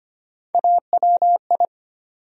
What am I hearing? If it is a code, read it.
Morse code, heard as a keyed tone, about 25 wpm, 700 Hz. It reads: AWI
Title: USA Call Sign Postfixes - 3 Letters 25wpm